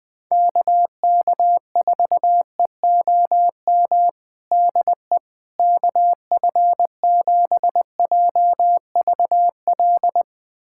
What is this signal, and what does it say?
Morse code, heard as a keyed tone, about 20 wpm, 700 Hz. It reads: KK4EOM DE KF7JVL
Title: USA Callsign DE Callsign - 6 Characters - 20wpm